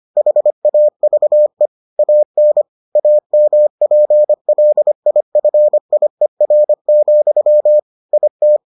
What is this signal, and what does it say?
Morse code, heard as a keyed tone, about 25 wpm, 600 Hz. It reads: HAVE AN AMPLIFIER, IT